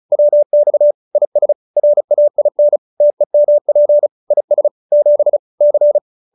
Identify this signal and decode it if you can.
Morse code, heard as a keyed tone, about 34 wpm, 590 Hz. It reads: WX IS RAIN TEMP IS 7 C